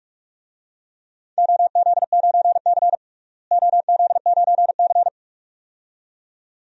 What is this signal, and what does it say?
Morse code, heard as a keyed tone, about 44 wpm, 700 Hz. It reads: OZ9C OZ9C
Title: DXCC-100 Call Signs - 4 Characters - Repeated 2x - 45wpm